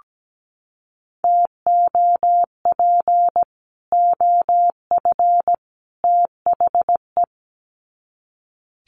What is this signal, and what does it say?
Morse code, heard as a keyed tone, about 17 wpm, 700 Hz. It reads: TOP OF THE
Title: Sets of 3 Words 17wpm